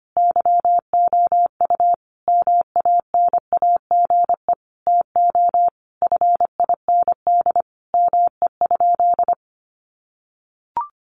Morse code, heard as a keyed tone, about 25 wpm, 700 Hz. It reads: YOU MANAGE TO FIND ME?
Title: Sentences from Top 1400 Words 25wpm